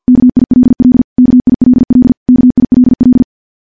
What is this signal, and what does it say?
Morse code, heard as a keyed tone, about 50 wpm, 250 Hz. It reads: KILL KILL KILL
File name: morse / morse-training-50WPM